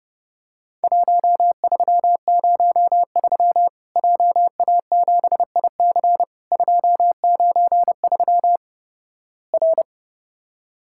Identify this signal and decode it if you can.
Morse code, heard as a keyed tone, about 30 wpm, 700 Hz. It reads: 1303 JA7IC 293 R
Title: Worked All Europe DX Contest - QTC Exchange - 30wpm